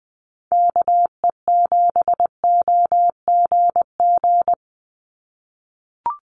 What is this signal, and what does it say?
Morse code, heard as a keyed tone, about 19 wpm, 700 Hz. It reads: KE7OGG
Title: USA Call Signs 20wpm